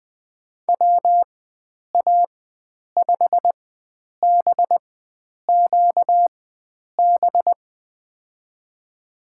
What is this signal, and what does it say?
Morse code, heard as a keyed tone, about 21 wpm, spaced as for 9 wpm, 700 Hz. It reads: WA5BQB
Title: USA Call Signs - 4x Character Spacing - 20wpm